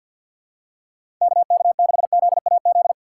Morse code, heard as a keyed tone, about 51 wpm, 700 Hz. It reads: KK6ZAZ
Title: USA Call Signs - 6 Characters - 3x Character Spacing - 50wpm